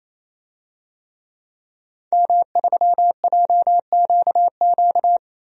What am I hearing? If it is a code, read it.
Morse code, heard as a keyed tone, about 27 wpm, 700 Hz. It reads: M3JQQ
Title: DXCC-100 Call Signs - 5 Characters - Repeated 3x - 28wpm